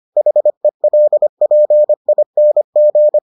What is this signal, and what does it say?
Morse code, heard as a keyed tone, about 25 wpm, 600 Hz. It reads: HELPING